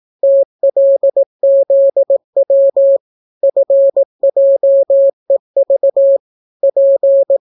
Morse code, heard as a keyed tone, about 18 wpm, 550 Hz. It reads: TLZW FJEV P